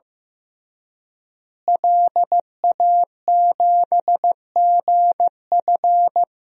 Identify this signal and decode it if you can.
Morse code, heard as a keyed tone, about 15 wpm, 700 Hz. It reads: LA7GF
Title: DXCC-100 Call Signs - 5 Characters - Repeated 3x - 15wpm